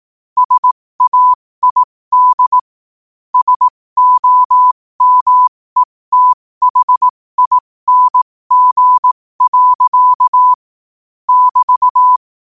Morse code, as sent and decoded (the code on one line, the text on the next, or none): ... .- .. -.. / ... --- -- . - .... .. -. --. .-.-.- / -...-
SAID SOMETHING. <BT>